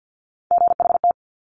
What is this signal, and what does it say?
Morse code, heard as a keyed tone, about 50 wpm, 700 Hz. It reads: GHT
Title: Top Three Letter Combinations in Words 50wpm